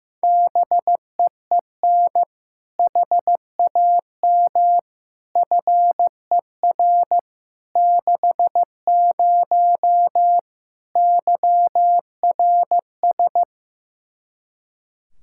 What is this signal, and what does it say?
Morse code, heard as a keyed tone, about 15 wpm, 700 Hz. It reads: BEEN HAM FER 60 YRS
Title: Practice QSO - Exchange 3 - More Info - 15wpm